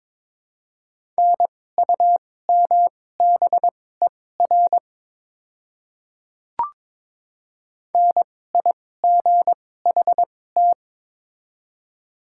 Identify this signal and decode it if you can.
Morse code, heard as a keyed tone, about 22 wpm, spaced as for 16 wpm, 700 Hz. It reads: NUMBER NIGHT
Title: 2x Character Spacing - Top 300 Words - 22wpm